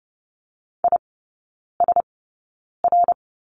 Morse code, heard as a keyed tone, about 30 wpm, spaced as for 9 wpm, 700 Hz. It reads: ISR
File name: Lesson 18e - Letter m - Three Characters 30wpm